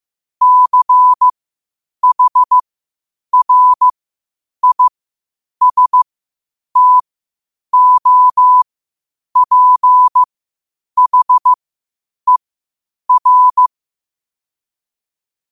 Morse code, heard as a keyed tone, about 15 wpm, spaced as for 8 wpm, 1000 Hz. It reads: CHRISTOPHER
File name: qotdfiles / 8wpm